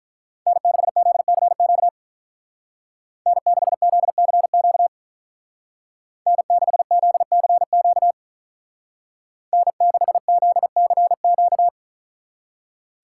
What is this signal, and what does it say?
Morse code, heard as a keyed tone, about 42 wpm, 700 Hz. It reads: N6ZCQ N6ZCQ N6ZCQ N6ZCQ
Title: Speed Racing - USA Call Signs 35wpm